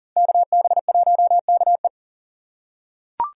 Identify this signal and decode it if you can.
Morse code, heard as a keyed tone, about 40 wpm, 700 Hz. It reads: KB1KE
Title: USA Call Signs - 5 Characters - 3x Character Spacing - 40wpm